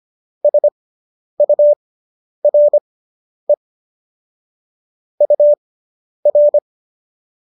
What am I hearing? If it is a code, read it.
Morse code, heard as a keyed tone, about 25 wpm, spaced as for 10 wpm, 590 Hz. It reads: SURE UR